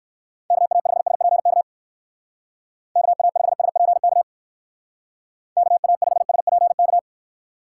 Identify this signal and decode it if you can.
Morse code, heard as a keyed tone, about 59 wpm, 700 Hz. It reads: KA5SPK KA5SPK KA5SPK